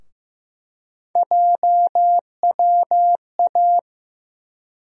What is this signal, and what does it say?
Morse code, heard as a keyed tone, about 15 wpm, 700 Hz. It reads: JWA
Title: USA Call Sign Postfixes - 3 Letters 15wpm